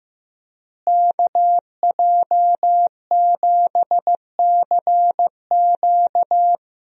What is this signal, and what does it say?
Morse code, heard as a keyed tone, about 16 wpm, 700 Hz. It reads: KJ7CQ
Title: USA Call Signs - 5 Characters - Repeated 3x - 15wpm